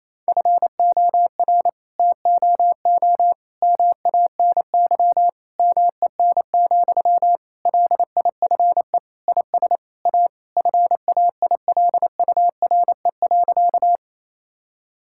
Morse code, heard as a keyed tone, about 27 wpm, 700 Hz. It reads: FOR TOO MANY MEN, LIFE IS A FAILURE.
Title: Sentences from Top 1000 Words - Review Part 6 - 28wpm